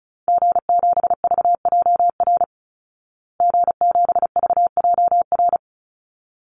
Morse code, heard as a keyed tone, about 34 wpm, 700 Hz. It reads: G7VJR G7VJR
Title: Call Signs Repeated 3x - 35wpm